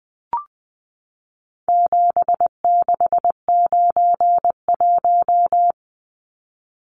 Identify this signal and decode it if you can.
Morse code, heard as a keyed tone, about 20 wpm, 700 Hz. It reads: 7691
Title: Numbers - 4-Digit 20wpm